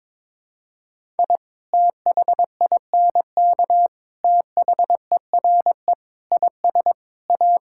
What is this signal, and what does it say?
Morse code, heard as a keyed tone, about 23 wpm, 700 Hz. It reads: I THINK THERE IS A
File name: practice / 22wpm / Build A Phrase - 5-Word Phrases - 22wpm